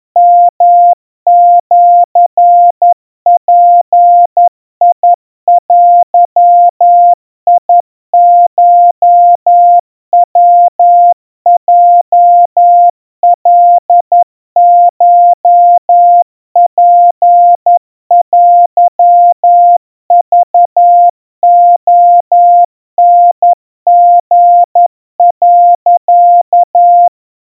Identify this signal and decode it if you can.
Morse code (Japanese Wabun code), heard as a keyed tone, about 11 wpm, 700 Hz. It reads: ヨシヅデコヤヲカコツテクレタリ、
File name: kawa / Morse sample / kamyu 40 cpm 700 Hz Crispy